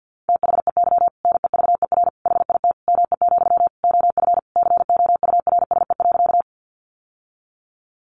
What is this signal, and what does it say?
Morse code, heard as a keyed tone, about 50 wpm, 700 Hz. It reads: THEY NEVER HIT ME, OF COURSE.
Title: Sentences from Top 1200 Words 50wpm